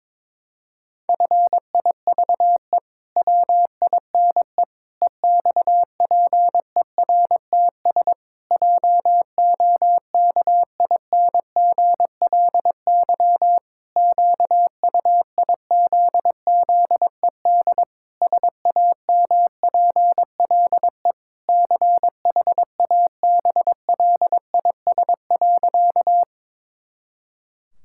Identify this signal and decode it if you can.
Morse code, heard as a keyed tone, about 22 wpm, 700 Hz. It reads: FIVE WINE EXPERTS JOKINGLY QUIZZED SAMPLE CHABLIS.
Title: Pangram Challenge v2 - 22wpm